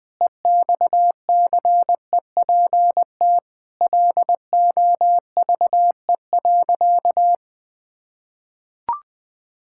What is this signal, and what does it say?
Morse code, heard as a keyed tone, about 21 wpm, 700 Hz. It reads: EXCEPT LOVE.
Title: Sentences from Top 1600 Words 20wpm